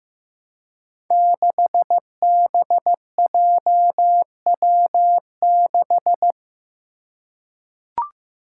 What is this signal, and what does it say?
Morse code, heard as a keyed tone, about 15 wpm, 700 Hz. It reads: 6BJW6
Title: Five Challenging Letter-Number Combinations 15wpm